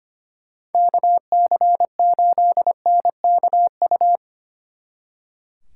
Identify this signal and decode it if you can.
Morse code, heard as a keyed tone, about 32 wpm, 700 Hz. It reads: KC8NKU